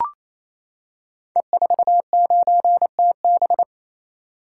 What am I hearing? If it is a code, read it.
Morse code, heard as a keyed tone, about 27 wpm, 700 Hz. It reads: E49TB
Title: Code Groups - 5 Letter:Numbers - 28wpm